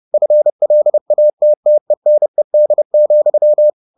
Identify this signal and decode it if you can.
Morse code, heard as a keyed tone, about 30 wpm, 600 Hz. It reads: FLATTENED,
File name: the old man and the sea / the old man and the sea 1 30wpm